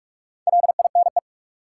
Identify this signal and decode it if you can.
Morse code, heard as a keyed tone, about 46 wpm, 700 Hz. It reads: LINE
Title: Sentences from Top 1000 Words - Review Part 1 - New Word Drill - 45wpm